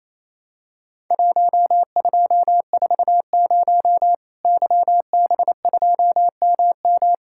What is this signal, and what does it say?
Morse code, heard as a keyed tone, about 28 wpm, 700 Hz. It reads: 1240 YB2MM
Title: Worked All Europe DX Contest - QTC Exchange - 28wpm